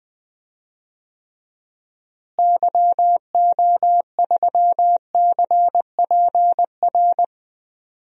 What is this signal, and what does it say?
Morse code, heard as a keyed tone, about 19 wpm, 700 Hz. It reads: YO3CPR
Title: DXCC-100 Call Signs - 6 Characters - Repeated 2x - 20wpm